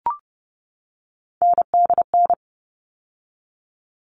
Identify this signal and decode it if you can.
Morse code, heard as a keyed tone, about 30 wpm, 700 Hz. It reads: NDN
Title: All Cut Numbers - 3-Digit 30wpm